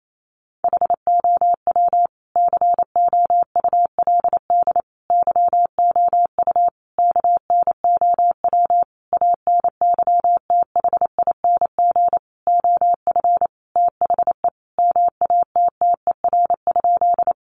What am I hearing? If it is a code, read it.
Morse code, heard as a keyed tone, about 28 wpm, 700 Hz. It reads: HOW COULD YOU KNOW ANYTHING OF THE MATTER?
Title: Sentences from Top 1000 Words 28wpm